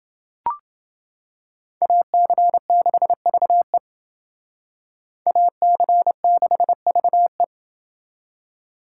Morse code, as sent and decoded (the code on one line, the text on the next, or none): .- -.-. -.... ...- . / .- -.-. -.... ...- .
AC6VE AC6VE